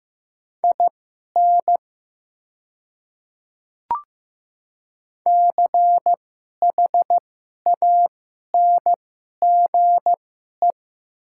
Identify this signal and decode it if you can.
Morse code, heard as a keyed tone, about 15 wpm, spaced as for 11 wpm, 690 Hz. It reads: IN CHANGE
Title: 2x Character Spacing - Top 500 Words - 15wpm